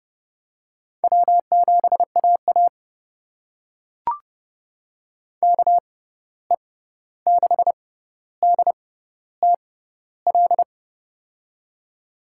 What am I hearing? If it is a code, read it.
Morse code, heard as a keyed tone, about 30 wpm, spaced as for 14 wpm, 700 Hz. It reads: W7AA KE6DTL
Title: USA Call Signs - 6x Character Spacing - 30wpm